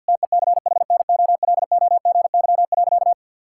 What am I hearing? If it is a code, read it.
Morse code, heard as a keyed tone, about 52 wpm, 700 Hz. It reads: TECHNOLOGY.
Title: Sentences from Top 1000 Words - Review Part 5 - 50wpm